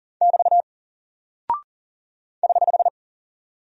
Morse code, heard as a keyed tone, about 40 wpm, 700 Hz. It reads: <BT> <HH>
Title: Single Character (Letter:Number:Punctuation:Prosign) Alt 40wpm